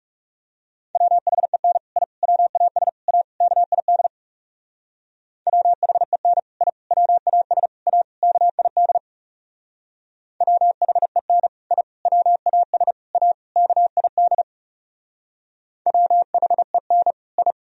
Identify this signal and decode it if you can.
Morse code, heard as a keyed tone, about 37 wpm, 700 Hz. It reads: WHEN I WAS A KID WHEN I WAS A KID WHEN I WAS A KID WHEN I
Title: Speed-Racing - Sets of 5 Words 30wpm